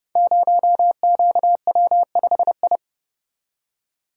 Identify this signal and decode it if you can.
Morse code, heard as a keyed tone, about 30 wpm, 700 Hz. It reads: 0QW5I
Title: Code Groups - 5 Letter:Numbers - 30wpm